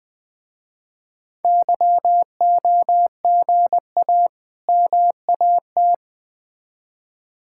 Morse code, heard as a keyed tone, about 20 wpm, 700 Hz. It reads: YOGA MAT
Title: Difficult Words with Hints - 20wpm